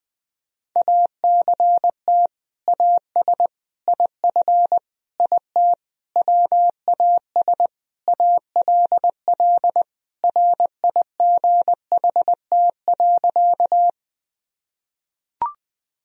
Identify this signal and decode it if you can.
Morse code, heard as a keyed tone, about 21 wpm, 700 Hz. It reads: ACT AS IF IT WAS ALL RIGHT.
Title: Sentences from Top 1000 Words - Review Part 1 - 20wpm